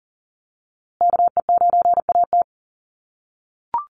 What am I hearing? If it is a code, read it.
Morse code, heard as a keyed tone, about 40 wpm, 700 Hz. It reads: KE9AT
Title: USA Call Signs Repeated 2x 40wpm